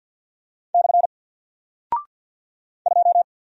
Morse code, heard as a keyed tone, about 48 wpm, 700 Hz. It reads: / 2
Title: Single Character (Letter:Number:Punctuation:Prosign) 50wpm